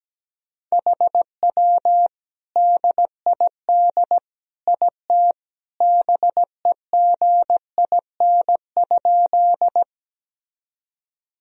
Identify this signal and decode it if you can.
Morse code, heard as a keyed tone, about 17 wpm, 700 Hz. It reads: HW DID IT BEGIN?
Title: Sentences from Top 300 Words and Abbreviations 17wpm